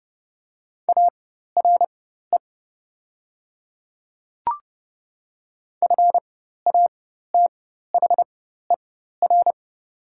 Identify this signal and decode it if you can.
Morse code, heard as a keyed tone, about 30 wpm, spaced as for 14 wpm, 700 Hz. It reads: ARE FATHER
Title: 4x Character Spacing - Top 200 Words - 30wpm